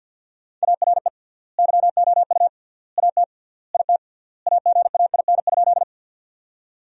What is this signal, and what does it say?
Morse code, heard as a keyed tone, about 50 wpm, 700 Hz. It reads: ARE YOU AT IT AGAIN?
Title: 3x Word-Spacing - Sentences from Top 200 Words - 50wpm